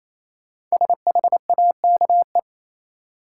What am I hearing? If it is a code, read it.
Morse code, heard as a keyed tone, about 28 wpm, 700 Hz. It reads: SHAKE
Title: Five Letter Words - 28wpm